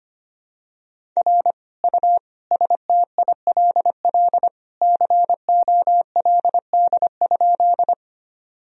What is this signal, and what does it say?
Morse code, heard as a keyed tone, about 25 wpm, 700 Hz. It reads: R U STILL COLD?
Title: Sentences from Top 400 Words and Abbreviations 25wpm